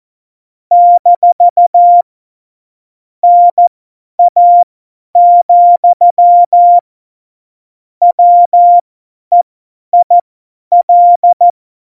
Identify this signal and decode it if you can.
Morse code, heard as a keyed tone, about 14 wpm, spaced as for 10 wpm, 700 Hz. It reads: - NA, WEIL
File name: Morsekurs / Witze / Witz 21 April 2020, 50 F70